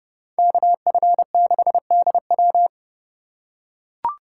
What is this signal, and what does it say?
Morse code, heard as a keyed tone, about 30 wpm, 700 Hz. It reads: KF6DW